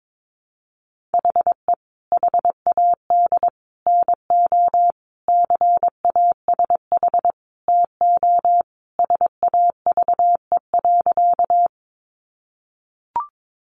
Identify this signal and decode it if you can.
Morse code, heard as a keyed tone, about 22 wpm, 700 Hz. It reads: HE HAD NO CASH TO SAVE.
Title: Sentences from Top 1100 Words 22wpm